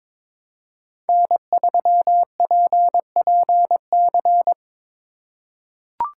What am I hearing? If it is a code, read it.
Morse code, heard as a keyed tone, about 22 wpm, 700 Hz. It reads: N3PPC
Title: USA Call Signs - 5 Characters - 4x Character Spacing - 22wpm